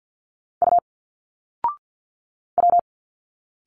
Morse code, heard as a keyed tone, about 49 wpm, 700 Hz. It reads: U W
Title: Single Letters 50wpm